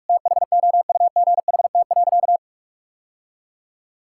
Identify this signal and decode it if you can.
Morse code, heard as a keyed tone, about 46 wpm, 700 Hz. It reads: THOUGHT.
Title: Sentences from Top 1000 Words - Review Part 5 - 45wpm